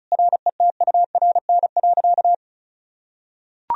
Morse code, heard as a keyed tone, about 34 wpm, 700 Hz. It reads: RETURN.
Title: Sentences from Top 1000 Words - Review Part 1 - 35wpm